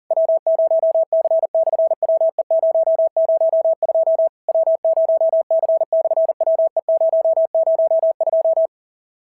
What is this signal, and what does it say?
Morse code, heard as a keyed tone, about 43 wpm, 650 Hz. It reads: W0C/WE002 W0C/WE002
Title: SOTA QSO Practice 40wpm